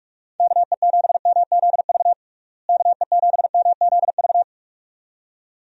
Morse code, heard as a keyed tone, about 43 wpm, 700 Hz. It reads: KE7MZV KE7MZV
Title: USA Call Signs - 6 Characters - Repeated 2x - 45wpm